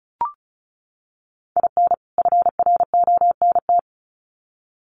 Morse code, heard as a keyed tone, about 34 wpm, 700 Hz. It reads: IN FRONT
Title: Sets of 2 Words 35wpm